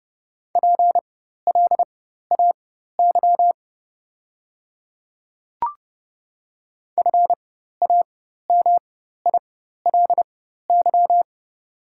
Morse code, heard as a keyed tone, about 30 wpm, spaced as for 14 wpm, 690 Hz. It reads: PLAY FAMILY
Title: 4x Character Spacing - Top 500 Words - 30wpm